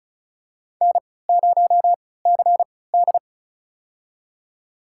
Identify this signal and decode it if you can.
Morse code, heard as a keyed tone, about 34 wpm, spaced as for 20 wpm, 700 Hz. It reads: N0CD
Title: USA Call Signs - 4 Characters - 3x Character Spacing - 35wpm